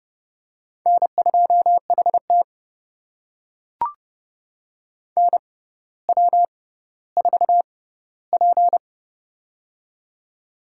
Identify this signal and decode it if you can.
Morse code, heard as a keyed tone, about 31 wpm, spaced as for 15 wpm, 700 Hz. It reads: N2HT NW4P
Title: USA Call Signs - 4 Characters - 6x Character Spacing - 30wpm